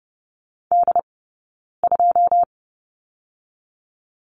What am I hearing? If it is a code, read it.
Morse code, heard as a keyed tone, about 30 wpm, 700 Hz. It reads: D 2
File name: course / 30:10wpm / 099 Lesson 17d - Letter u - Two Characters 30wpm